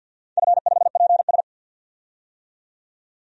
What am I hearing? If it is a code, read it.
Morse code, heard as a keyed tone, about 53 wpm, 700 Hz. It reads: F5PS
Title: DXCC-100 Call Signs - 4 Characters - 50wpm